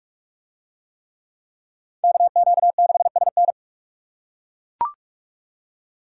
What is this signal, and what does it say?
Morse code, heard as a keyed tone, about 41 wpm, 700 Hz. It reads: KQ6SN